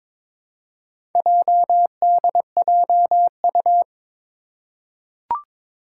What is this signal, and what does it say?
Morse code, heard as a keyed tone, about 22 wpm, 700 Hz. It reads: JDJU
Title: Four Challenging Letter-Number Combinations 22wpm